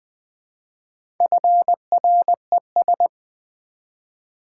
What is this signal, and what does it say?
Morse code, heard as a keyed tone, about 20 wpm, 700 Hz. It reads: FRES
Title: Build A Word - 1 Letter At A Time - 5-Letter Words - 20wpm